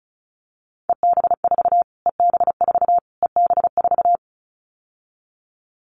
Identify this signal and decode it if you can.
Morse code, heard as a keyed tone, about 34 wpm, 700 Hz. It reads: EB4 EB4 EB4
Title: Callsign Prefixes for Top 25 DXCC - Repeated 3x - 35wpm